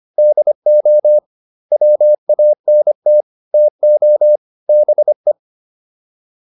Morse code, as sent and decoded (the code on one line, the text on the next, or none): -.. --- / .-- .- -. - / - --- / -... .
DO WANT TO BE